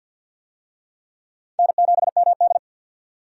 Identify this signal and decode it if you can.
Morse code, heard as a keyed tone, about 50 wpm, 700 Hz. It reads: N7MD
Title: USA Call Signs - 4 Characters - 50wpm